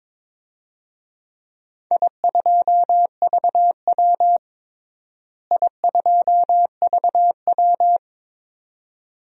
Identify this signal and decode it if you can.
Morse code, heard as a keyed tone, about 22 wpm, 700 Hz. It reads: I2VW I2VW
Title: DXCC-100 Call Signs - 4 Characters - Repeated 2x - 22wpm